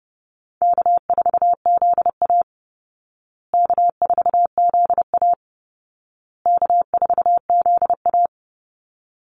Morse code, heard as a keyed tone, about 30 wpm, 700 Hz. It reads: K4ZA K4ZA K4ZA
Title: Easy Call Signs Repeated 3x - 30wpm